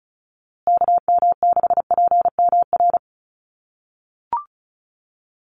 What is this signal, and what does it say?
Morse code, heard as a keyed tone, about 34 wpm, 700 Hz. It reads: KM6PMR
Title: USA Call Signs Repeated 2x 35wpm